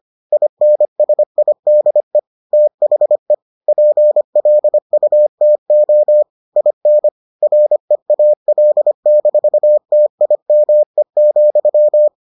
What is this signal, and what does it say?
Morse code, heard as a keyed tone, about 25 wpm, 600 Hz. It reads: INSIDE THE PLUTO IN REAL-TIME,